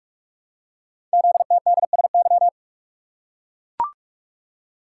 Morse code, heard as a keyed tone, about 44 wpm, 700 Hz. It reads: ZTDSY
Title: Code Groups - 5 Letters - 45wpm